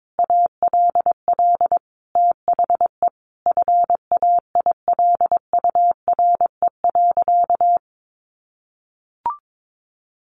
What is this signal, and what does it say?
Morse code, heard as a keyed tone, about 22 wpm, 700 Hz. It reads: ALL THE FAILURE.
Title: Sentences from Top 900 Words 22wpm